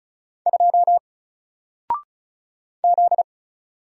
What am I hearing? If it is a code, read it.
Morse code, heard as a keyed tone, about 34 wpm, 700 Hz. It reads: 2 Z
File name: Single-Letter-Number-35wpm